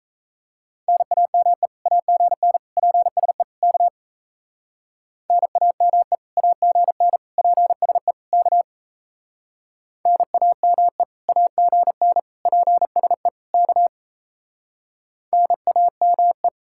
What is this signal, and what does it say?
Morse code, heard as a keyed tone, about 35 wpm, 700 Hz. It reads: NAME AGN PSE K NAME AGN PSE K NAME AGN PSE K NAME